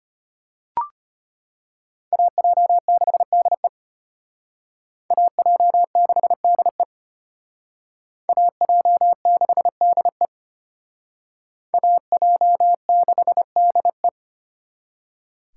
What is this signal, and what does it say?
Morse code, heard as a keyed tone, about 33 wpm, 700 Hz. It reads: AJ6DE AJ6DE AJ6DE AJ6DE